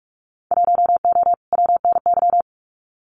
Morse code, heard as a keyed tone, about 44 wpm, 700 Hz. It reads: 1O WNY
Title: Winter Field Day Exchanges 45wpm